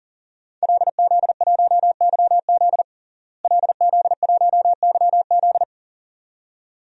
Morse code, heard as a keyed tone, about 37 wpm, 700 Hz. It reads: LZ1YZ LZ1YZ
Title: DXCC-100 Call Signs - 5 Characters - Repeated 2x - 40wpm